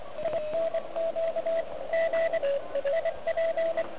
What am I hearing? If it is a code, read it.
Morse code, heard as a keyed tone, about 24 wpm, 660 Hz, with radio noise. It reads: INQ G IP
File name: ARCHIV / OK QRP 2021 / REC07